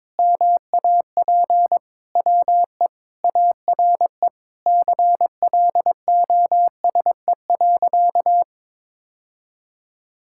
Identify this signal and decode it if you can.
Morse code, heard as a keyed tone, about 22 wpm, 700 Hz. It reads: MAP WE ARE CLOSE.
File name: practice / 22wpm / Sentences with 2-Word Phrases - 22wpm